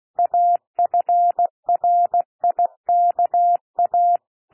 Morse code, heard as a keyed tone, about 16 wpm, 700 Hz. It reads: AFRIKA
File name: cwmp3 / P1 SOUND PASSAGE 16 wpm